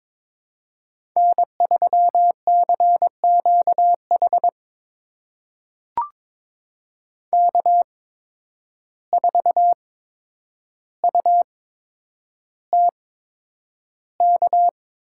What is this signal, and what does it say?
Morse code, heard as a keyed tone, about 22 wpm, spaced as for 9 wpm, 700 Hz. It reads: N3CQH K4UTK
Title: USA Call Signs - 5 Characters - 8x Character Spacing - 22wpm